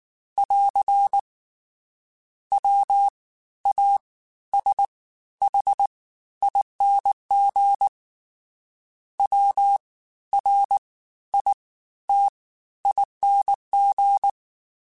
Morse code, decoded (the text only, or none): <AR> WASHING WRITING